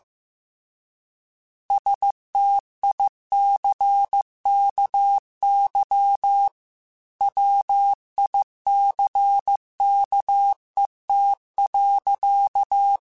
Morse code, heard as a keyed tone, about 15 wpm, 800 Hz. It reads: STICKY WICKET.
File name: Misc / Misc-mp3s / Word Pairs @13